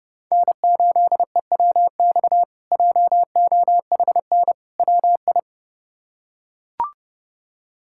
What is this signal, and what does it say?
Morse code, heard as a keyed tone, about 30 wpm, 700 Hz. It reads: N8EWX JOHN WI
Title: K1USN SST QSO Practice - 30wpm